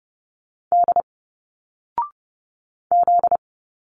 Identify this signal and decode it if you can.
Morse code, heard as a keyed tone, about 30 wpm, 700 Hz. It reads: D Z
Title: Single Letters 30wpm